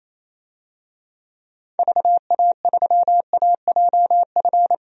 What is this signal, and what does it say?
Morse code, heard as a keyed tone, about 27 wpm, 700 Hz. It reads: VA3AJF
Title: DXCC-100 Call Signs - 6 Characters - Repeated 3x - 28wpm